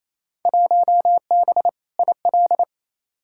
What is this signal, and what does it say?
Morse code, heard as a keyed tone, about 28 wpm, 700 Hz. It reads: 1B IL
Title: Field Day Exchanges Encore - 28wpm